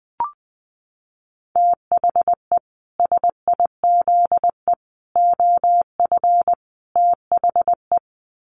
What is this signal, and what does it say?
Morse code, heard as a keyed tone, about 20 wpm, 700 Hz. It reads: THE SIZE OF THE
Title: Sets of 4 Words 20wpm